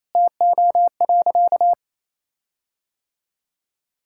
Morse code, heard as a keyed tone, about 28 wpm, 700 Hz. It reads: TO.
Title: Sentences from Top 1600 Words 28wpm